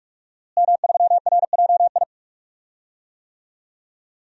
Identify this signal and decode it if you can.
Morse code, heard as a keyed tone, about 46 wpm, 700 Hz. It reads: M3RJI
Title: DXCC-100 Call Signs - 5 Characters - Repeated 2x - 45wpm